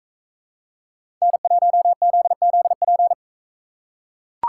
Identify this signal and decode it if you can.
Morse code, heard as a keyed tone, about 41 wpm, 700 Hz. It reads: N1ZZP